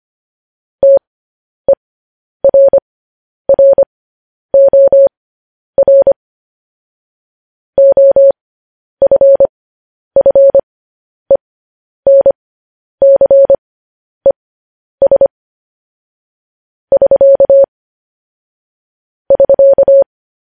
Morse code, heard as a keyed tone, about 25 wpm, spaced as for 10 wpm, 550 Hz. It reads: TERROR OFFENCES <SK> <SK>
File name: podcast / News headlines-1x-25x10